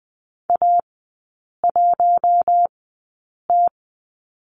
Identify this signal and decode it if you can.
Morse code, heard as a keyed tone, about 20 wpm, spaced as for 8 wpm, 700 Hz. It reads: A1T
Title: 041 Lesson 8e - Number 1 - Three Characters 20wpm